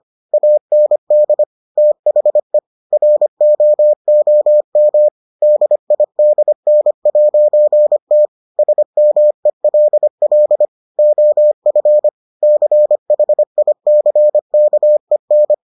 Morse code, heard as a keyed tone, about 25 wpm, 600 Hz. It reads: AND THE ROOM DIDN'T SMELL OF CHICKEN